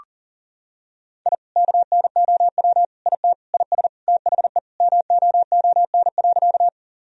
Morse code, heard as a keyed tone, about 40 wpm, 700 Hz. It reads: I KNOW IT IS THE MOON.
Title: Sentences from Top 1000 Words - Review Part 3 - 40wpm